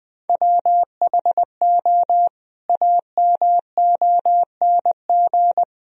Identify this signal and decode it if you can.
Morse code, heard as a keyed tone, about 21 wpm, 700 Hz. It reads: WHO AMONG
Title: Sentences from Top 1000 Words - Review Part 3 - 20wpm